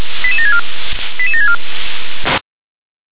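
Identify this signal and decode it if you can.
Five-tone selective call (ZVEI), heard as quick runs of five tones, tones of about 80 ms. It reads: 88764 88764
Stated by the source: ZVEI five-tone selective call received on SDR.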